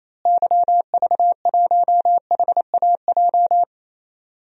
Morse code, heard as a keyed tone, about 28 wpm, 700 Hz. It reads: YV1HAJ
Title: DXCC-100 Call Signs - 6 Characters - Repeated 2x - 28wpm